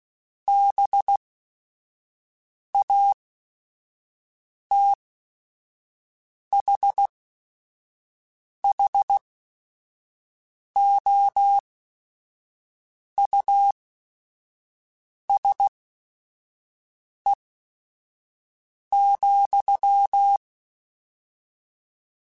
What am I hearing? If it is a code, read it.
Morse code, heard as a keyed tone, about 16 wpm, spaced as for 5 wpm, 800 Hz. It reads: BATHHOUSE,
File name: General / General-mp3s / Houses @5